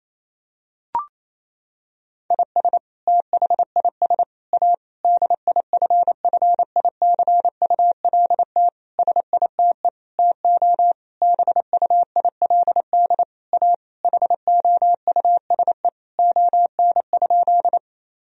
Morse code, heard as a keyed tone, about 28 wpm, 700 Hz. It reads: IS THIS A DIFFICULT SITE TO BUILD A HOUSE ON?
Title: Sentences from Top 1000 Words - Review Part 4 - 28wpm